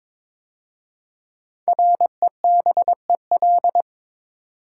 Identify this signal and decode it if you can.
Morse code, heard as a keyed tone, about 23 wpm, 700 Hz. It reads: REBEL
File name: 5 Letter Words - First 3 Letters Given - 22wpm